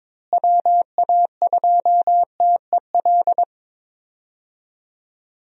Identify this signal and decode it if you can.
Morse code, heard as a keyed tone, about 22 wpm, 700 Hz. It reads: WA2TEL
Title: USA Call Signs - 6 Characters - Repeated 2x - 22wpm